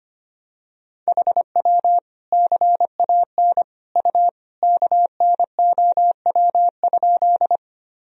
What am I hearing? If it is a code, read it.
Morse code, heard as a keyed tone, about 25 wpm, 700 Hz. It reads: HW CAN U KNOW?
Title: Sentences from Top 100 Words and Abbreviations 25wpm